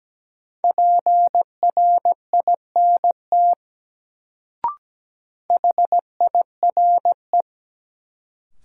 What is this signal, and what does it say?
Morse code, heard as a keyed tone, about 17 wpm, 700 Hz. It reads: PRINT HIRE
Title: Sentences from Top 1400 Words - New Word Drill - 17wpm